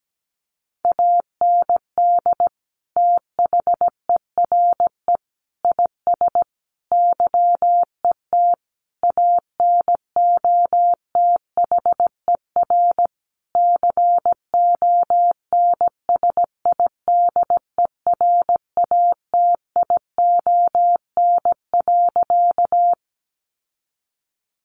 Morse code, heard as a keyed tone, about 17 wpm, 700 Hz. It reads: AND THERE IS YET ANOTHER CONSIDERATION.
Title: Sentences from Top 1500 Words 17wpm